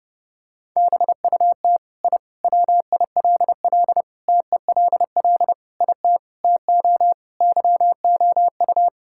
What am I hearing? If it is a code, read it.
Morse code, heard as a keyed tone, about 32 wpm, 700 Hz. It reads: BUT I WILL TELL IT TO YOU
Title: Sentences from Top 1000 Words - Review Part 5 - 30wpm